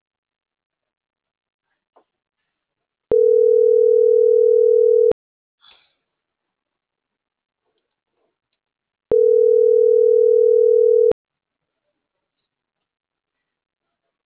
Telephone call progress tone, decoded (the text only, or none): ringback tone